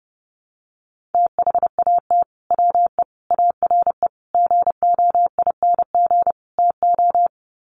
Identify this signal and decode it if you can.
Morse code, heard as a keyed tone, about 30 wpm, 700 Hz. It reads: THAT WE ARE GOING TO
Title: Sets of 5 Words Encore 30wpm